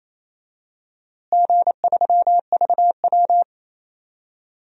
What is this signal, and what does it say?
Morse code, heard as a keyed tone, about 28 wpm, 700 Hz. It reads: G3VW
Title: DXCC-100 Call Signs - 4 Characters - Repeated 3x - 28wpm